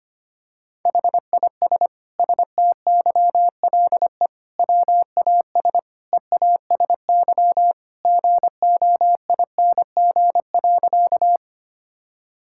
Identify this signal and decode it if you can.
Morse code, heard as a keyed tone, about 25 wpm, 700 Hz. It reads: HIS STYLE WAS EASY GOING.
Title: Sentences from Top 1000 Words - Review Part 5 - 25wpm